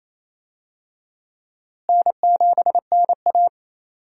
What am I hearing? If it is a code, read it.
Morse code, heard as a keyed tone, about 28 wpm, 700 Hz. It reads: N7NA
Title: USA Call Signs - 4 Characters - Repeated 3x - 28wpm